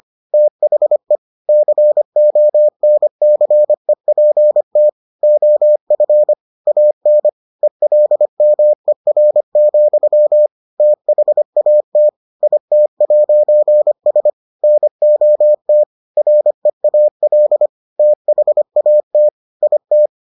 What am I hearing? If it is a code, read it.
Morse code, heard as a keyed tone, about 25 wpm, 600 Hz. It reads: THE CONCEPT OF AN ELMER, THAT IT'S NOT REAL THAT IT